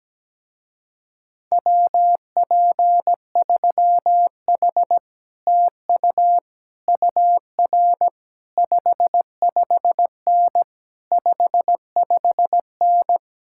Morse code, heard as a keyed tone, about 17 wpm, 700 Hz. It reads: WP3H TU UR 55N 55N
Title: Speed-Racing Common QSO Phrases - 17wpm